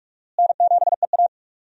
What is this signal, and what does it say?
Morse code, heard as a keyed tone, about 47 wpm, 700 Hz. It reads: N7EA
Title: USA Call Signs - 4 Characters - Repeated 2x - 45wpm